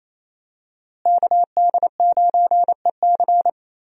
Morse code, heard as a keyed tone, about 28 wpm, 700 Hz. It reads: KD9EC
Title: USA Call Signs - 5 Characters - Repeated 2x - 28wpm